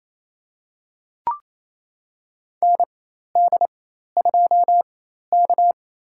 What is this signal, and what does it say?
Morse code, heard as a keyed tone, about 28 wpm, spaced as for 13 wpm, 700 Hz. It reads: ND2K
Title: USA Call Signs - 4 Characters - 4x Character Spacing - 28wpm